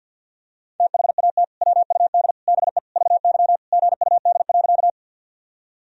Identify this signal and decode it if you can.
Morse code, heard as a keyed tone, about 50 wpm, 700 Hz. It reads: THAT WUD BE VY GUD.
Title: Sentences from Top 200 Words and Abbreviations 50wpm